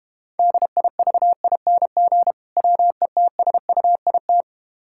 Morse code, heard as a keyed tone, about 31 wpm, 700 Hz. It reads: DIVING WETSUIT
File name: practice / 25wpm / Difficult Words - Speed-Racing with Hints - 25wpm